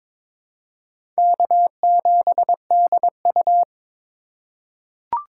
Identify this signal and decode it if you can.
Morse code, heard as a keyed tone, about 22 wpm, 700 Hz. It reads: K7DU
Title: USA Call Signs - 4 Characters - 2x Character Spacing - 22wpm